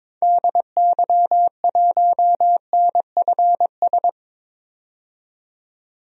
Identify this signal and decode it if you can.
Morse code, heard as a keyed tone, about 22 wpm, 700 Hz. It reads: DY1NFS
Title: DXCC-100 Call Signs - 6 Characters - 22wpm